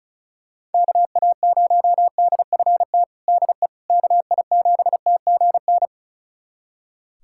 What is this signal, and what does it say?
Morse code, heard as a keyed tone, about 34 wpm, 700 Hz. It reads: KA0DFT DE KI7TGN
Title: USA Callsign DE Callsign - 6 Characters - 35wpm